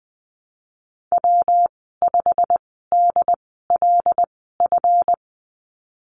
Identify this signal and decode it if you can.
Morse code, heard as a keyed tone, about 20 wpm, spaced as for 15 wpm, 700 Hz. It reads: W5DLF
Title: 147 Lesson 24g - Letter y - Call Signs 20wpm